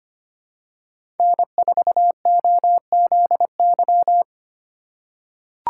Morse code, heard as a keyed tone, about 25 wpm, 700 Hz. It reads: N4OZY
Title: USA Call Signs - 5 Characters - 2x Character Spacing - 25wpm